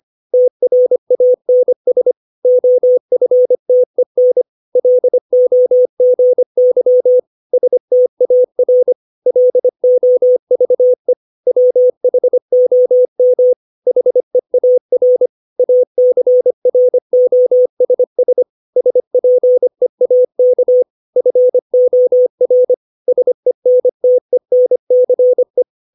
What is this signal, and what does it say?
Morse code, heard as a keyed tone, about 25 wpm, 500 Hz. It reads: TRANS OFTEN LOGY STAR LOVE WHOM HEAR ACROSS SPEAK FOR SENTENCE
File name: 632 words / 632 Words 25WPM